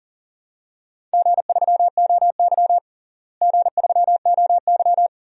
Morse code, heard as a keyed tone, about 40 wpm, 700 Hz. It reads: G3OY G3OY
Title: DXCC-100 Call Signs - 4 Characters - Repeated 3x - 40wpm